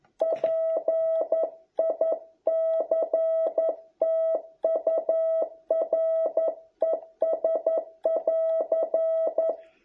Morse code, heard as a keyed tone, about 11 wpm, 650 Hz. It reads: PICTURES <AR>